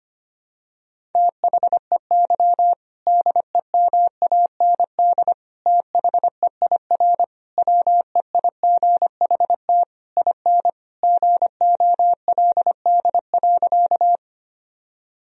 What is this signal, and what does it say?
Morse code, heard as a keyed tone, about 25 wpm, 700 Hz. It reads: THEY DEMAND THEIR WEIGHT IN GOLD.
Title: Sentences from Top 1000 Words - Review Part 3 - 25wpm